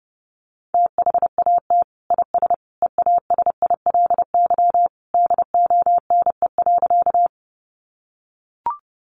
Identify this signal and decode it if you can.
Morse code, heard as a keyed tone, about 30 wpm, 700 Hz. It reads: THAT IS EASILY DONE.
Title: Sentences from Top 900 Words 30wpm